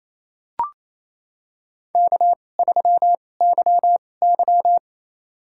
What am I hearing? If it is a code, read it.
Morse code, heard as a keyed tone, about 28 wpm, spaced as for 20 wpm, 700 Hz. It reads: K3YY
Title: USA Call Signs - 4 Characters - 2x Character Spacing - 28wpm